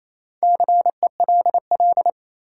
Morse code, heard as a keyed tone, about 28 wpm, 700 Hz. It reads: CELL
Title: Sentences from Top 1100 Words - New Word Drill - 28wpm